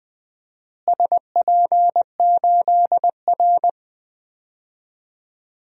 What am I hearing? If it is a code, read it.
Morse code, heard as a keyed tone, about 20 wpm, 700 Hz. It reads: SP8R
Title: DXCC-100 Call Signs - 4 Characters - Repeated 3x - 20wpm